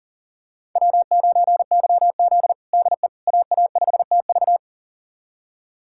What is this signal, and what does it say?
Morse code, heard as a keyed tone, about 40 wpm, 700 Hz. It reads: W9YZ DE AA5TV
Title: USA Callsign DE Callsign 40wpm